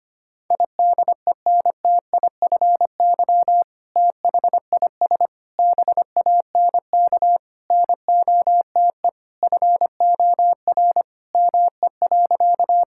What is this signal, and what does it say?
Morse code, heard as a keyed tone, about 25 wpm, 700 Hz. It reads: IDENTIFY THIS BANK NOTE FOR ME.
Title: Sentences from Top 1000 Words - Review Part 6 - 25wpm